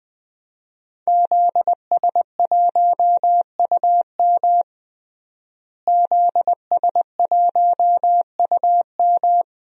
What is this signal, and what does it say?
Morse code, heard as a keyed tone, about 21 wpm, 700 Hz. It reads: ZS1UM ZS1UM
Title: DXCC-100 Call Signs - 5 Characters - Repeated 3x - 20wpm